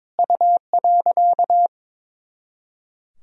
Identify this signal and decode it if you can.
Morse code, heard as a keyed tone, about 23 wpm, 700 Hz. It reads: U.
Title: Sentences from Top 400 Words and Abbreviations 22wpm